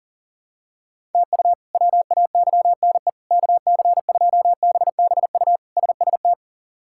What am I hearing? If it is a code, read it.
Morse code, heard as a keyed tone, about 40 wpm, 700 Hz. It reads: TU WAYNE KC2BBU SST
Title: K1USN SST QSO Practice - 40wpm